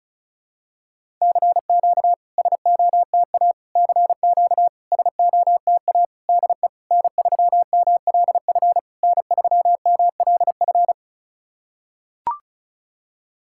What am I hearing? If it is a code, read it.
Morse code, heard as a keyed tone, about 36 wpm, 700 Hz. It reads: CQ SOTA CQ SOTA DE N3MLF N3MLF
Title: SOTA QSO Practice 35wpm